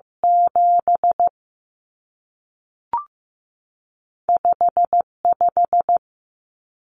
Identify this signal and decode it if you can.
Morse code, heard as a keyed tone, about 15 wpm, 700 Hz. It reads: 7 5 5
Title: Numbers - 2-Digit 15wpm